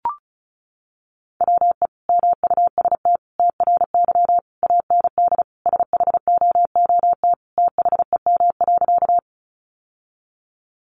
Morse code, heard as a keyed tone, about 34 wpm, 700 Hz. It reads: WE MUST TRY AND SHOOT THEM.
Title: Sentences from Top 1200 Words 35wpm